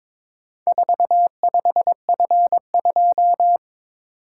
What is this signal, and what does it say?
Morse code, heard as a keyed tone, about 22 wpm, 690 Hz. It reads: 45F2
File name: practice / 22wpm / Four Letter-Number Combinations - 22wpm